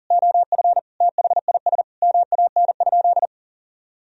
Morse code, heard as a keyed tone, about 39 wpm, 700 Hz. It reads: OF THIS MAN?
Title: Sentences from Top 1000 Words - Review Part 3 - 40wpm